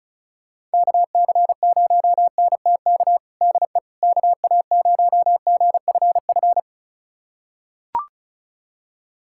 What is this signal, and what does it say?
Morse code, heard as a keyed tone, about 36 wpm, 700 Hz. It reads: KC0NTK DE KA0GFF
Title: USA Callsign DE Callsign 35wpm